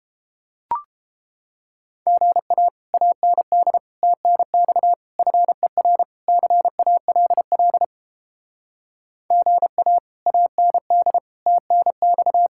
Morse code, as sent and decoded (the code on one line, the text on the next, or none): --. .- / .- -. -.. / - -. -..- / ..-. .-. / -.-. .- .-.. .-.. / --. .- / .- -. -.. / - -. -..-
GA AND TNX FER CALL GA AND TNX